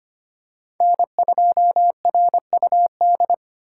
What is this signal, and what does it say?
Morse code, heard as a keyed tone, about 25 wpm, 700 Hz. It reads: N2RUD
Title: USA Call Signs - 5 Characters - 3x Character Spacing - 25wpm